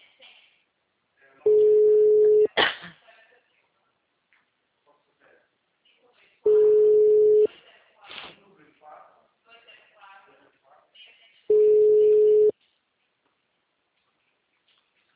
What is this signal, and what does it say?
Ringback tone (Europe, 425 Hz), with speech.
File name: out- -2002- - - .23114